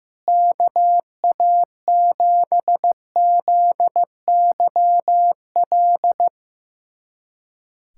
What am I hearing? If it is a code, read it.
Morse code, heard as a keyed tone, about 15 wpm, 700 Hz. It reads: KA7ZYL
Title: USA Callsign DE Callsign - 6 Characters - 15wpm